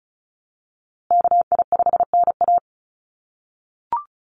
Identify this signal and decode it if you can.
Morse code, heard as a keyed tone, about 34 wpm, 700 Hz. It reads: KI5NA
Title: USA Call Signs Repeated 3x 35wpm